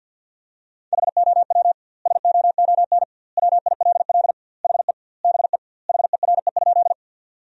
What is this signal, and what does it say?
Morse code, heard as a keyed tone, about 50 wpm, 700 Hz. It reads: HOW SOON WILL HE BE HERE?
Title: 2x Word-Spacing - Sentences from Top 400 Words - 50wpm